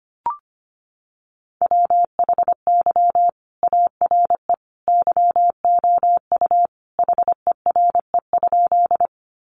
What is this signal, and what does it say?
Morse code, heard as a keyed tone, about 25 wpm, 700 Hz. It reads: WHY ARE YOU HERE?
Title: Sentences from Top 200 Words 25wpm